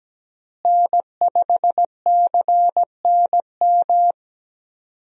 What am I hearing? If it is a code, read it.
Morse code, heard as a keyed tone, about 17 wpm, 700 Hz. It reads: N5CNM
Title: USA Call Signs - 5 Characters - Repeated 2x - 17wpm